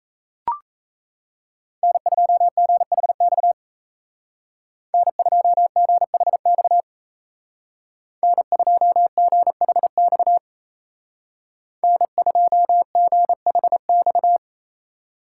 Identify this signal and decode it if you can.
Morse code, heard as a keyed tone, about 33 wpm, 700 Hz. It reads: N2GHX N2GHX N2GHX N2GHX
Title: Speed Racing - USA Call Signs 28wpm